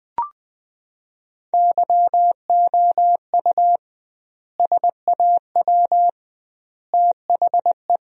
Morse code, heard as a keyed tone, about 20 wpm, 700 Hz. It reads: YOU SAW THE
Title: 2x Word-Spacing - Sentences from Top 400 Words - 20wpm